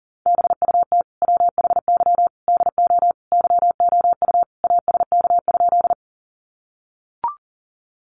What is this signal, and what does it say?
Morse code, heard as a keyed tone, about 40 wpm, 700 Hz. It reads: BUT WHY DO YOU ASK?
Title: Sentences from Top 200 Words 40wpm